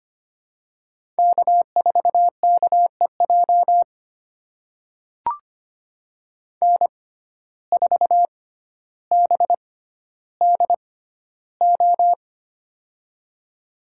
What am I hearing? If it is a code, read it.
Morse code, heard as a keyed tone, about 25 wpm, spaced as for 13 wpm, 700 Hz. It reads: K4KEJ N4BDO
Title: USA Call Signs - 5 Characters - 6x Character Spacing - 25wpm